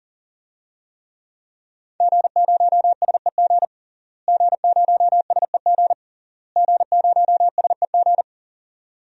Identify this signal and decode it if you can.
Morse code, heard as a keyed tone, about 37 wpm, 700 Hz. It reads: G0SEG G0SEG G0SEG
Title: DXCC-100 Call Signs - 5 Characters - Repeated 3x - 40wpm